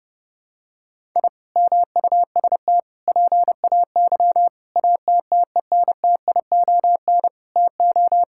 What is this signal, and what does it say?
Morse code, heard as a keyed tone, about 32 wpm, 700 Hz. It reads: I MUST PAY ATTENTION TO
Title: Sentences from Top 1000 Words - Review Part 6 - 30wpm